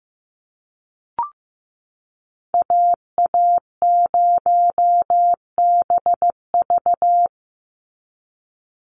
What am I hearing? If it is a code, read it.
Morse code, heard as a keyed tone, about 15 wpm, 700 Hz. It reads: AA0BV
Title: NA SOTA Activator Callsigns 15wpm